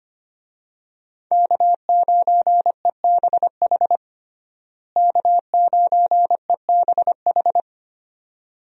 Morse code, heard as a keyed tone, about 25 wpm, 700 Hz. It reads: K9EBH K9EBH
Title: USA Call Signs - 5 Characters - Repeated 2x - 25wpm